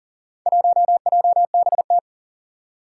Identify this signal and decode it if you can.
Morse code, heard as a keyed tone, about 40 wpm, 700 Hz. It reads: 1JBT
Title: Four Challenging Letter-Number Combinations 40wpm